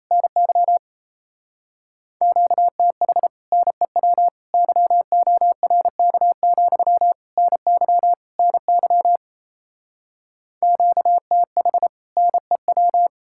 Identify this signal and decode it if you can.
Morse code, heard as a keyed tone, about 38 wpm, 700 Hz. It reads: NY QTH NEW YORK, NY NY QTH NEW